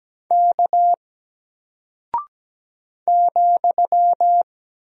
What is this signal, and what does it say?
Morse code, heard as a keyed tone, about 17 wpm, 700 Hz. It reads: K ,
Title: Single Character (Letter:Number:Punctuation:Prosign) Alt 17wpm